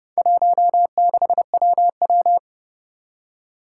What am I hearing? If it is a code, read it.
Morse code, heard as a keyed tone, about 30 wpm, 700 Hz. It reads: 16WW
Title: Four Challenging Letter-Number Combinations 30wpm